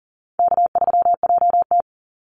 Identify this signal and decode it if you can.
Morse code, heard as a keyed tone, about 40 wpm, 700 Hz. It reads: K3JT
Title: Easy Call Signs - 40wpm